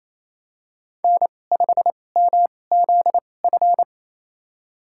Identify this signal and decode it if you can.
Morse code, heard as a keyed tone, about 27 wpm, spaced as for 20 wpm, 700 Hz. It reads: N5MZF
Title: USA Call Signs - 5 Characters - 2x Character Spacing - 28wpm